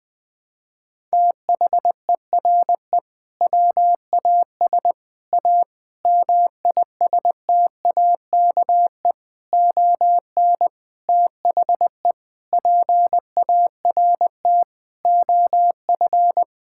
Morse code, heard as a keyed tone, about 19 wpm, 700 Hz. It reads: THERE WAS A MISTAKE ON THE PART OF
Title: Sentences with 5-Word Phrases - 20wpm